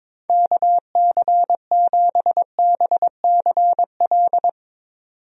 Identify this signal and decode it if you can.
Morse code, heard as a keyed tone, about 22 wpm, 700 Hz. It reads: KC7BCL
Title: USA Call Signs - 6 Characters - 22wpm